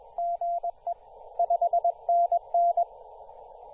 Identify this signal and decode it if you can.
Morse code, heard as a keyed tone, about 21 wpm, 690 Hz, with radio noise. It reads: GE 5NN
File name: t2r-18-071mhz-cw